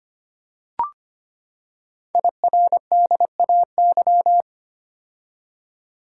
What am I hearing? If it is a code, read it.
Morse code, heard as a keyed tone, about 25 wpm, 700 Hz. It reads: IRDAY
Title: Code Groups - 5 Letters - 25wpm